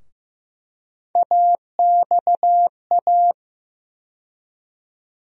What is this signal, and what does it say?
Morse code, heard as a keyed tone, about 15 wpm, 700 Hz. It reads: AXA
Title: USA Call Sign Postfixes - 3 Letters 15wpm